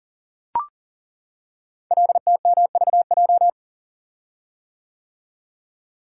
Morse code, heard as a keyed tone, about 40 wpm, 700 Hz. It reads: LTMVJ
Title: Code Groups - 5 Letters - 40wpm